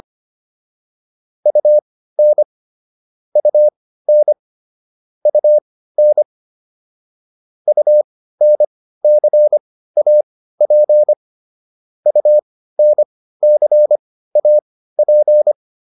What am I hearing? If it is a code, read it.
Morse code, heard as a keyed tone, about 25 wpm, spaced as for 15 wpm, 600 Hz. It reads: UN UN UN UNCAP UNCAP